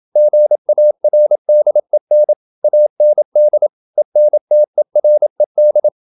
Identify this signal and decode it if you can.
Morse code, heard as a keyed tone, about 27 wpm, 600 Hz. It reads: GARDEN AND ENTERED
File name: Never 0027wpm